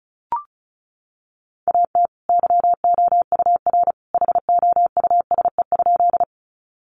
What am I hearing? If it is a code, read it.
Morse code, heard as a keyed tone, about 34 wpm, 700 Hz. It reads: AT YOUR HOUSE?
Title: Sentences from Top 200 Words 35wpm